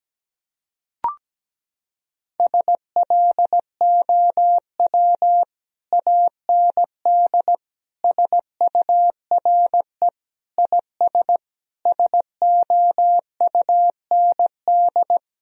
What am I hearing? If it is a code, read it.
Morse code, heard as a keyed tone, about 17 wpm, 700 Hz. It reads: SLOW AND SURE IS SOUND
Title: Sentences from Top 1000 Words - Review Part 2 - 17wpm